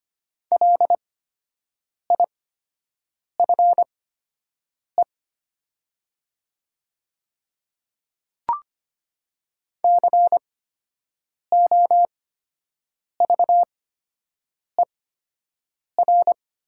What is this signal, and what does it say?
Morse code, heard as a keyed tone, about 25 wpm, spaced as for 7 wpm, 700 Hz. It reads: LIFE COVER
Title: 8x Character Spacing - Top 300 Words - 25wpm